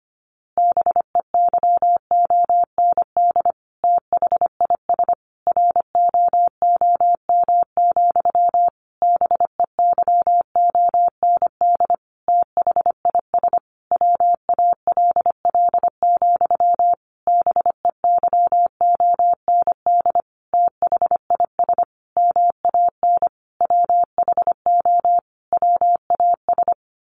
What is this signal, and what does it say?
Morse code, heard as a keyed tone, about 25 wpm, 700 Hz. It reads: BEYOND THIS ROOM, BEYOND THIS WALL, BEYOND THIS MAN WHO WAS
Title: A Sound Of Thunder 25wpm